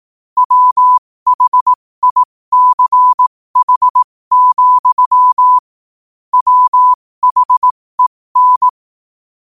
Morse code, heard as a keyed tone, about 18 wpm, spaced as for 15 wpm, 1000 Hz. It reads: WHICH, WHEN
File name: qotdfiles / 15wpm